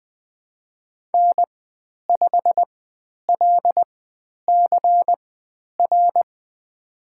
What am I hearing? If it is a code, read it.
Morse code, heard as a keyed tone, about 20 wpm, spaced as for 10 wpm, 700 Hz. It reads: N5LCR
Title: 102 Lesson 17g - Letter u - Call Signs 20wpm